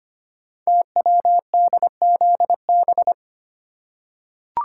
Morse code, heard as a keyed tone, about 25 wpm, 700 Hz. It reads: TWDZB
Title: Code Groups - 5 Letters - 25wpm